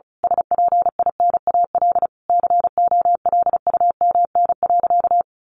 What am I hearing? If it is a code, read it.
Morse code, heard as a keyed tone, about 34 wpm, 700 Hz. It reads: SPINAL COLUMN.
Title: A Sound Of Thunder 35wpm